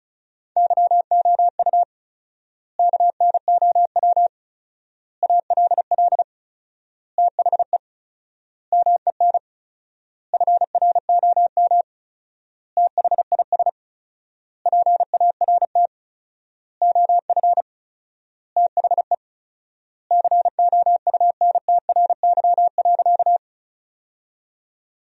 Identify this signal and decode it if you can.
Morse code, heard as a keyed tone, about 36 wpm, 700 Hz. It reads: YOU KNOW ALL THE MEN FROM THIS PART OF THE COUNTRY.
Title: 4x Word-Spacing - Sentences from Top 300 Words - 35wpm